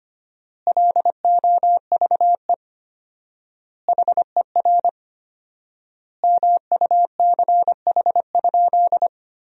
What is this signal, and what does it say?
Morse code, heard as a keyed tone, about 25 wpm, 700 Hz. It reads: LOVE HER MUCH?
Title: 4x Word-Spacing - Sentences from Top 400 Words - 25wpm